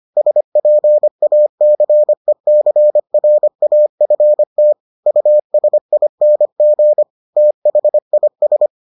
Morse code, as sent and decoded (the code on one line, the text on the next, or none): ... .--. .- -.-. -.-. .-. .- ..-. - / ..- ... .. -. --. / - .... .. ...
SPACECRAFT USING THIS